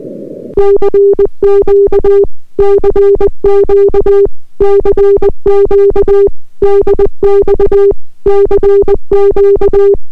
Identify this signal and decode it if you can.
Morse code, heard as a keyed tone, about 22 wpm, 400 Hz. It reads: CQ CQ CQ DX CQ